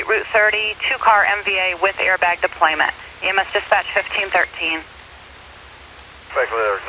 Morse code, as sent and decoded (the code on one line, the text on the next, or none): none